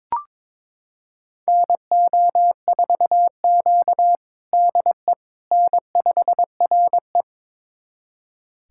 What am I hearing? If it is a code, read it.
Morse code, heard as a keyed tone, about 22 wpm, 700 Hz. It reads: NO4Q DE N5RE
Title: USA Callsign DE Callsign - 4 Characters - 22wpm